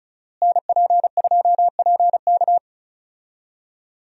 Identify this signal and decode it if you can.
Morse code, heard as a keyed tone, about 34 wpm, 700 Hz. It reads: NP2PK
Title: USA Call Signs - 5 Characters - Repeated 2x - 35wpm